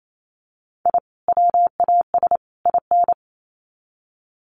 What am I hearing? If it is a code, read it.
Morse code, heard as a keyed tone, about 28 wpm, 700 Hz. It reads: I WAS IN
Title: Sets of 3 Words 28wpm